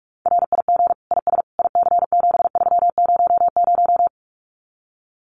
Morse code, heard as a keyed tone, about 44 wpm, 700 Hz. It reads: RIG IS IC7300
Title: Common QSO Phrases Encore 45wpm